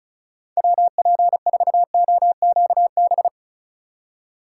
Morse code, heard as a keyed tone, about 45 wpm, 700 Hz. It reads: WP4OQB